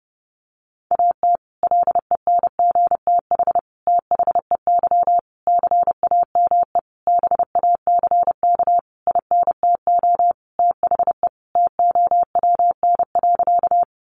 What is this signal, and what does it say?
Morse code, heard as a keyed tone, about 30 wpm, 700 Hz. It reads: AT LENGTH THEY CAME BACK INTO THE TOWN.
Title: Sentences from Top 1000 Words 30wpm